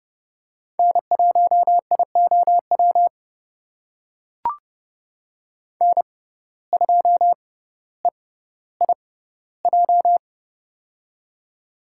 Morse code, heard as a keyed tone, about 30 wpm, spaced as for 15 wpm, 700 Hz. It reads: N1IOW N2EIJ
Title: USA Call Signs - 5 Characters - 6x Character Spacing - 30wpm